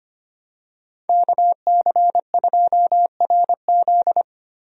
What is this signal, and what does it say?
Morse code, heard as a keyed tone, about 25 wpm, 700 Hz. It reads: KC2RZ
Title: USA Call Signs - 5 Characters - Repeated 2x - 25wpm